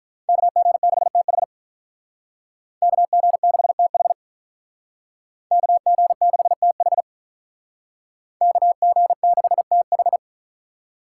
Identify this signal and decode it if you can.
Morse code, heard as a keyed tone, about 43 wpm, 700 Hz. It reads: KG6TH KG6TH KG6TH KG6TH
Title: Speed Racing - USA 2x2 Call Signs 35wpm